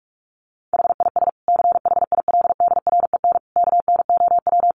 Morse code, heard as a keyed tone, about 44 wpm, 700 Hz. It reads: HIS CHILDREN KNOW
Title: Sentences from Top 900 Words 45wpm